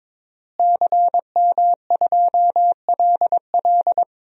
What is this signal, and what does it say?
Morse code, heard as a keyed tone, about 22 wpm, 700 Hz. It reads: CM2LL
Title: DXCC-100 Call Signs - 5 Characters - Repeated 2x - 22wpm